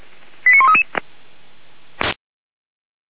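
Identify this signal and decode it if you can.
Five-tone selective call (ZVEI), heard as quick runs of five tones, tones of about 70 ms. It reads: 89133
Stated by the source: ZVEI five-tone call received by SDR.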